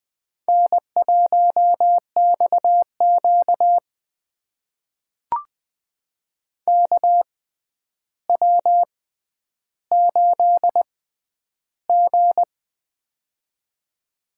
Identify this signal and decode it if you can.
Morse code, heard as a keyed tone, about 20 wpm, spaced as for 10 wpm, 700 Hz. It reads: N1XQ KW8G
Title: USA Call Signs - 4 Characters - 6x Character Spacing - 20wpm